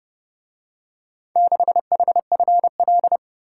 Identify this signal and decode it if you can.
Morse code, heard as a keyed tone, about 30 wpm, 700 Hz. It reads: 6HFL